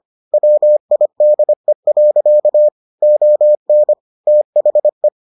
Morse code, heard as a keyed tone, about 25 wpm, 600 Hz. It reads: WIDE. ON THE